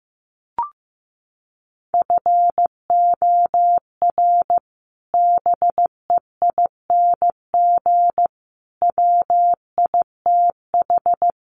Morse code, heard as a keyed tone, about 14 wpm, 700 Hz. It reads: FOR BEING WITH
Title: Sets of 4 Words 15wpm